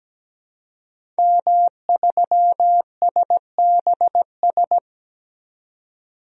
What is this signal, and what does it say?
Morse code, heard as a keyed tone, about 17 wpm, 700 Hz. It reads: M3SBS
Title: DXCC-100 Call Signs - 5 Characters - 17wpm